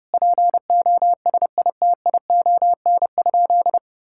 Morse code, heard as a keyed tone, about 32 wpm, 700 Hz. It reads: POSITION?
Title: Sentences from Top 1000 Words - Review Part 5 - 30wpm